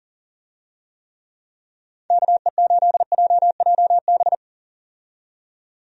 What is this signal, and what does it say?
Morse code, heard as a keyed tone, about 39 wpm, 700 Hz. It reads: KE8JJB
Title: USA Call Signs - 6 Characters - 40wpm